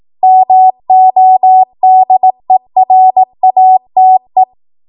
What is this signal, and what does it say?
Morse code, heard as a keyed tone, about 18 wpm, 750 Hz. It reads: MODERATE
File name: Morse / 18WPM